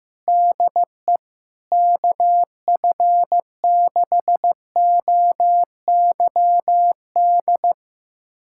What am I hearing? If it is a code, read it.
Morse code, heard as a keyed tone, about 15 wpm, 700 Hz. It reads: DE KF6OYD
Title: USA Callsign DE Callsign - 6 Characters - 15wpm